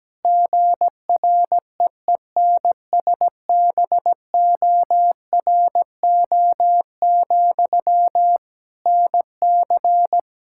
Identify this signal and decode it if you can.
Morse code, heard as a keyed tone, about 17 wpm, 700 Hz. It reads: GREENSBORO, NC
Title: Practice QSO - Exchange 1 - Beginning - 17wpm